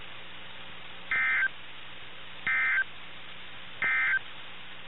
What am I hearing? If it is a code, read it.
Emergency Alert System (SAME) tones, heard as harsh buzzing bursts. It is heard from 1.0 to 4.3 s.
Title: NOAA Weather Radio - KIH61